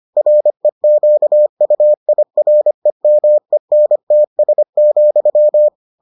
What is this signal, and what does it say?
Morse code, heard as a keyed tone, about 25 wpm, 600 Hz. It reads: REQUIREMENTS,